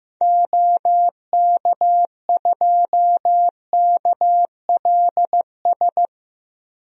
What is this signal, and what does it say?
Morse code, heard as a keyed tone, about 15 wpm, 700 Hz. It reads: OK2KLS
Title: DXCC-100 Call Signs - 6 Characters - Repeated 3x - 15wpm